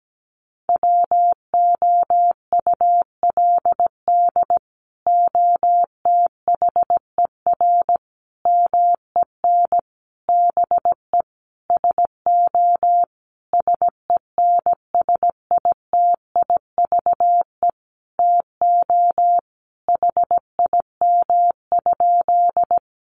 Morse code, heard as a keyed tone, about 17 wpm, 700 Hz. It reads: WOULD OTHER MEN BE SO SENSITIVE TO HIM?
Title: Sentences from Top 1400 Words 17wpm